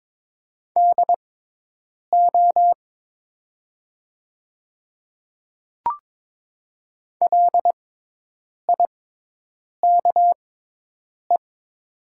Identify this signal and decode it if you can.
Morse code, heard as a keyed tone, about 22 wpm, spaced as for 8 wpm, 700 Hz. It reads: DO LIKE
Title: 6x Character Spacing - Top 100 Words - 22wpm